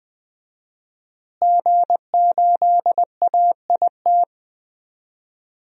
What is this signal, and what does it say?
Morse code, heard as a keyed tone, about 20 wpm, 700 Hz. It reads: G8AIT
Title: DXCC-100 Call Signs - 5 Characters - Repeated 3x - 20wpm